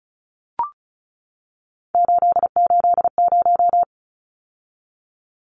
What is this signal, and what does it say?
Morse code, heard as a keyed tone, about 34 wpm, 700 Hz. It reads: 880
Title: Numbers - 3-Digit 35wpm